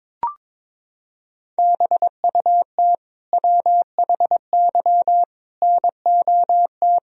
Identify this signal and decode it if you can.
Morse code, heard as a keyed tone, about 22 wpm, 700 Hz. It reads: BUT WHY NOT
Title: Sentences from Top 1000 Words - Review Part 1 - 22wpm